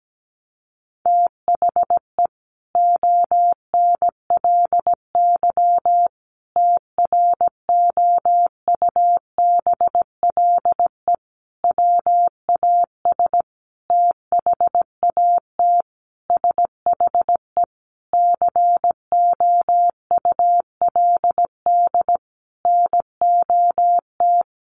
Morse code, heard as a keyed tone, about 17 wpm, 700 Hz. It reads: THE ONLY TROUBLE WAS THAT SHE COULD NOT
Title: Sentences from Top 1200 Words 17wpm